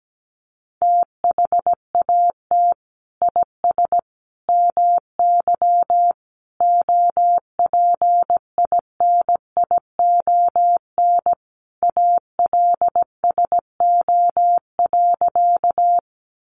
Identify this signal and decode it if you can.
Morse code, heard as a keyed tone, about 17 wpm, 700 Hz. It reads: THAT IS MY OPINION ALSO.
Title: Sentences from Top 1200 Words 17wpm